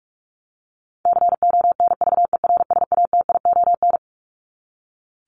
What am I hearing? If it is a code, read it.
Morse code, heard as a keyed tone, about 44 wpm, 700 Hz. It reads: CONVERSATION
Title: Top 1751-2000 Words 45wpm